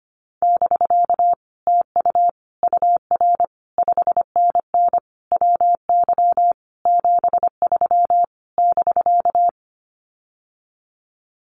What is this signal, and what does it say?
Morse code, heard as a keyed tone, about 25 wpm, 700 Hz. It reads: <BK> TU UR 5NN WY 73 <BK>
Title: Common QSO Phrases Encore 25wpm